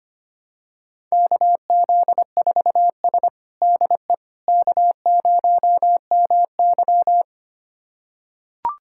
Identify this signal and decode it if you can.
Morse code, heard as a keyed tone, about 25 wpm, 700 Hz. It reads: KZ4S DE K0MY
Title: USA Callsign DE Callsign - 4 Characters - 25wpm